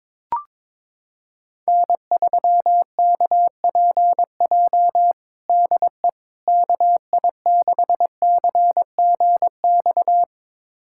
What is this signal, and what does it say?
Morse code, heard as a keyed tone, about 22 wpm, 700 Hz. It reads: N3KPJ DE KI6CGX
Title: USA Callsign DE Callsign 22wpm